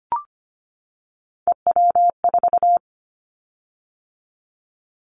Morse code, heard as a keyed tone, about 25 wpm, 690 Hz. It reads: EW4
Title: Callsign Prefixes for Top 50 DXCC - 25wpm